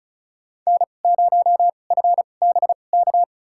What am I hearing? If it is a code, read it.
Morse code, heard as a keyed tone, about 34 wpm, spaced as for 25 wpm, 700 Hz. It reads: N0FBK
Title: USA Call Signs - 5 Characters - 2x Character Spacing - 35wpm